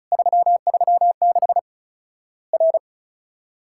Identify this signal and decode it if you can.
Morse code, heard as a keyed tone, about 34 wpm, 700 Hz. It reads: 336 R
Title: Worked All Europe DX Contest - QTC Exchange - 35wpm